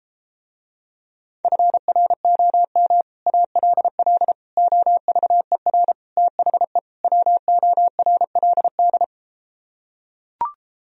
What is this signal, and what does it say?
Morse code, heard as a keyed tone, about 33 wpm, 700 Hz. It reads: FROM ALL OVER THE WORLD